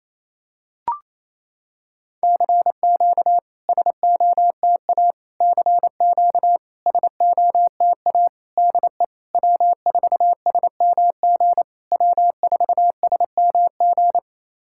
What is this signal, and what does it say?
Morse code, heard as a keyed tone, about 28 wpm, 700 Hz. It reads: CQ SOTA CQ SOTA DE W4SMG W4SMG
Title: SOTA QSO Practice 28wpm